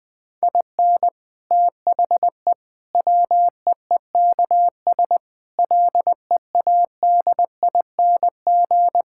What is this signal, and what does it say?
Morse code, heard as a keyed tone, about 20 wpm, 700 Hz. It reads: IN THE WEEKS LEADING
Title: Sentences with 7-Word Phrases - Spoken Beginning - 20wpm